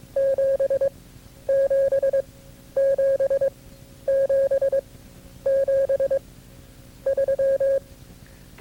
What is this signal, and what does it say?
Morse code, heard as a keyed tone, about 22 wpm, spaced as for 11 wpm, 540 Hz. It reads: 77777 3
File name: CW-practice/KB6MT-tape2-side1